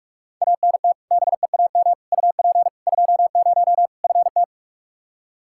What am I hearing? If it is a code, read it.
Morse code, heard as a keyed tone, about 46 wpm, 700 Hz. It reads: ANT BEAM UP 20 FT
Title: Practice QSO - Exchange 2 - Info - 45wpm